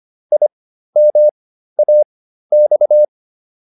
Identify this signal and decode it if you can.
Morse code, heard as a keyed tone, about 25 wpm, spaced as for 13 wpm, 600 Hz. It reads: IMAX